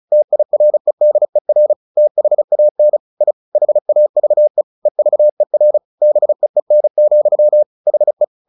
Morse code, heard as a keyed tone, about 34 wpm, 600 Hz. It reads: TIREDER THAN I HAVE EVER BEEN, HE
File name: the old man and the sea 2 35wpm